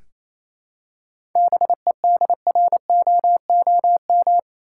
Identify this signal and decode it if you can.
Morse code, heard as a keyed tone, about 28 wpm, 700 Hz. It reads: BEDROOM
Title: Seven Letter Words - 28wpm